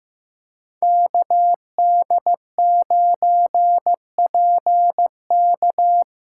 Morse code, heard as a keyed tone, about 15 wpm, 700 Hz. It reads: KD9PK
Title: USA Call Signs - 5 Characters - 2x Character Spacing - 15wpm